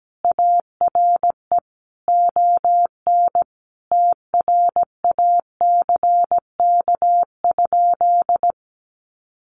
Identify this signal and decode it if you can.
Morse code, heard as a keyed tone, about 17 wpm, 700 Hz. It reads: ARE ON TRACK?
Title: Sentences from Top 1200 Words 17wpm